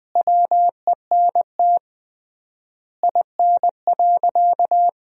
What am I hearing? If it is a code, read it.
Morse code, heard as a keyed tone, about 21 wpm, 700 Hz. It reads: WENT IN.
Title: 3x Word-Spacing - Sentences from Top 300 Words - 20wpm